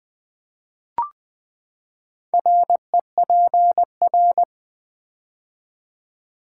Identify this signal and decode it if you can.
Morse code, heard as a keyed tone, about 20 wpm, 700 Hz. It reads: REPR
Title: Build A Word - 4 Letters At A Time - 12-Letter Words - 20wpm